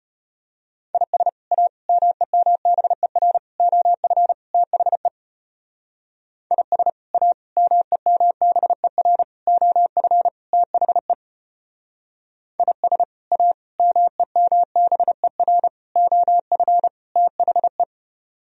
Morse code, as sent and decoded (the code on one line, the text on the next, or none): .. ... / .- / -- . -- -... .-. / --- ..-. / - .... / .. ... / .- / -- . -- -... .-. / --- ..-. / - .... / .. ... / .- / -- . -- -... .-. / --- ..-. / - .... .
IS A MEMBER OF THE IS A MEMBER OF THE IS A MEMBER OF THE